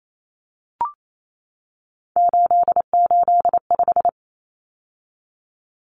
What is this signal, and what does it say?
Morse code, heard as a keyed tone, about 28 wpm, 700 Hz. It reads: 885
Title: Numbers - 3-Digit 28wpm